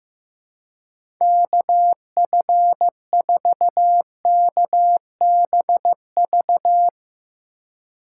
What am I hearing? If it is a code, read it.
Morse code, heard as a keyed tone, about 15 wpm, 700 Hz. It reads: KF4KBV
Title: USA Call Signs - 6 Characters - 2x Character Spacing - 15wpm